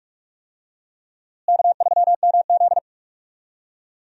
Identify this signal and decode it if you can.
Morse code, heard as a keyed tone, about 46 wpm, 700 Hz. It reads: K3MZ
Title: USA Call Signs - 4 Characters - Repeated 2x - 45wpm